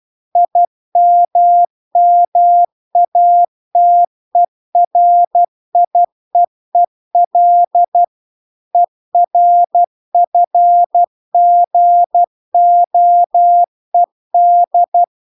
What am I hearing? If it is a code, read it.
Morse code, heard as a keyed tone, about 12 wpm, 700 Hz. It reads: IMMATERIEEL ERFGOED
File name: cw-cursus-VERON-a03